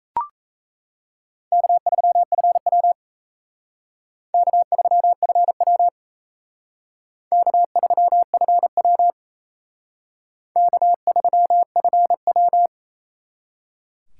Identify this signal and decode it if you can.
Morse code, heard as a keyed tone, about 35 wpm, 700 Hz. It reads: K3FW K3FW K3FW K3FW
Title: Speed Racing - USA Call Signs 28wpm